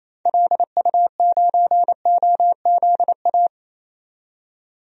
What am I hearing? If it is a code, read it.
Morse code, heard as a keyed tone, about 27 wpm, 700 Hz. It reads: LU9OZA
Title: DXCC-100 Call Signs - 6 Characters - 28wpm